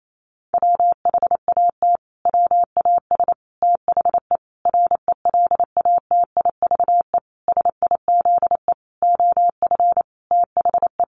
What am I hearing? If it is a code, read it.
Morse code, heard as a keyed tone, about 28 wpm, 700 Hz. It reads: WHAT WAS THE RELATIVE SIZE OF THE
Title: Sentences from Top 1400 Words 28wpm